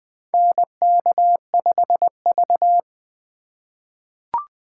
Morse code, heard as a keyed tone, about 21 wpm, 700 Hz. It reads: NK5V
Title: USA Call Signs - 4 Characters - 3x Character Spacing - 20wpm